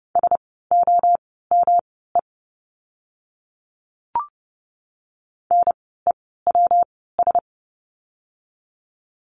Morse code, heard as a keyed tone, about 30 wpm, spaced as for 17 wpm, 700 Hz. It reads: SOME NEWS
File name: course / 30:10wpm / 115 Lesson 19f - Letter w - Words 30wpm